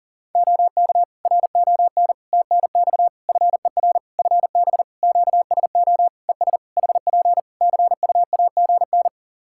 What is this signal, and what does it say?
Morse code, heard as a keyed tone, about 40 wpm, 700 Hz. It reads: OK RON TNX FER FB QSO ES HP CUAGN
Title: Practice QSO - Exchange 4 - Ending - 40wpm